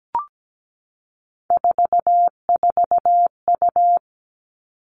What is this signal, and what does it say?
Morse code, heard as a keyed tone, about 17 wpm, 700 Hz. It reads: 44U
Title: Three Challenging Letter-Number Combinations 17wpm